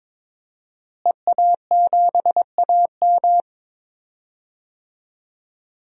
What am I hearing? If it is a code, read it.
Morse code, heard as a keyed tone, about 22 wpm, 700 Hz. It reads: EA7AM
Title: DXCC-100 Call Signs - 5 Characters - Repeated 2x - 22wpm